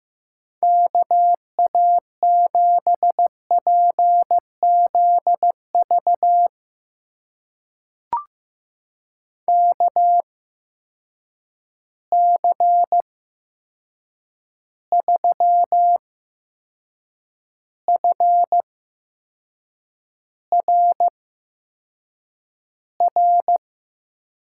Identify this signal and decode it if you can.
Morse code, heard as a keyed tone, about 15 wpm, spaced as for 6 wpm, 700 Hz. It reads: KA7PZV KC3FRR
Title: USA Call Signs - 6 Characters - 8x Character Spacing - 15wpm